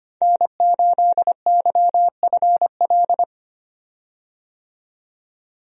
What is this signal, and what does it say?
Morse code, heard as a keyed tone, about 25 wpm, 700 Hz. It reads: N8YFL
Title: USA Call Signs - 5 Characters - 25wpm